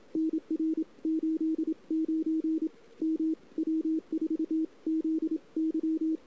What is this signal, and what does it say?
Morse code, heard as a keyed tone, about 27 wpm, 320 Hz. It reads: NR 89 M W4ZY
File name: CW / 2008-ss-cw-k5tr-op-n5rz -03:00